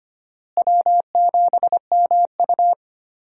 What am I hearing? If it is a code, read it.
Morse code, heard as a keyed tone, about 25 wpm, 700 Hz. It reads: W7MU
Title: USA Call Signs - 4 Characters - Repeated 2x - 25wpm